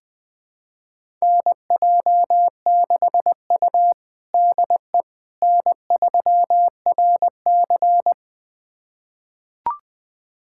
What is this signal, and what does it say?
Morse code, heard as a keyed tone, about 20 wpm, 700 Hz. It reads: NJ6U DE N3RC
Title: USA Callsign DE Callsign - 4 Characters - 20wpm